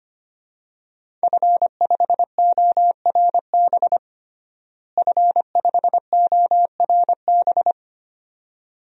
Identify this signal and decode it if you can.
Morse code, heard as a keyed tone, about 25 wpm, 700 Hz. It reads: F5ORB F5ORB
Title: DXCC-100 Call Signs - 5 Characters - Repeated 2x - 25wpm